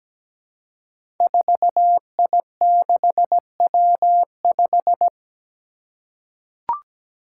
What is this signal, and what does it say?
Morse code, heard as a keyed tone, about 17 wpm, 700 Hz. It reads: 4I6W5
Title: Five Challenging Letter-Number Combinations 17wpm